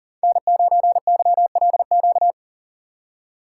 Morse code, heard as a keyed tone, about 38 wpm, 700 Hz. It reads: N9YLQ
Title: USA Call Signs - 5 Characters - 4x Character Spacing - 40wpm